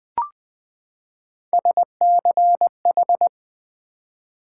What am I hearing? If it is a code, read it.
Morse code, heard as a keyed tone, about 20 wpm, 700 Hz. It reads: SCH
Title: Build A Word - 3 Letters At A Time - 6-Letter Words - 20wpm